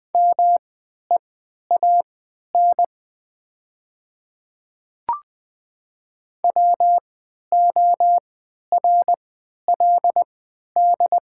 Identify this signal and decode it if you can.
Morse code, heard as a keyed tone, about 20 wpm, spaced as for 11 wpm, 700 Hz. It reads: MEAN WORLD
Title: 3x Character Spacing - Top 200 Words - 20wpm